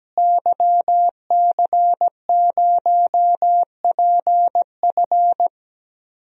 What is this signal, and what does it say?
Morse code, heard as a keyed tone, about 17 wpm, 700 Hz. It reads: YC0PF
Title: DXCC-100 Call Signs - 5 Characters - Repeated 2x - 17wpm